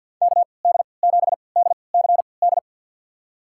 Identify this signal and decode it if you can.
Morse code, heard as a keyed tone, about 51 wpm, spaced as for 28 wpm, 690 Hz. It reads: KD7DCD
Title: USA Call Signs - 6 Characters - 3x Character Spacing - 50wpm